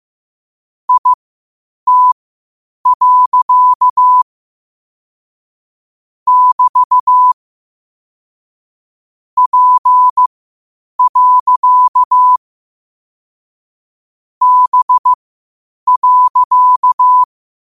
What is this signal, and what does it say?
Morse code, heard as a keyed tone, about 15 wpm, spaced as for 8 wpm, 1000 Hz. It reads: IT. <BT> P. B.